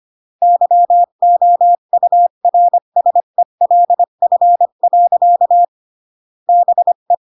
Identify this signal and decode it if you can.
Morse code, heard as a keyed tone, about 25 wpm, 700 Hz. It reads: YOURSELF. BE
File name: CW / jodorowsky / 82 commandments 25-20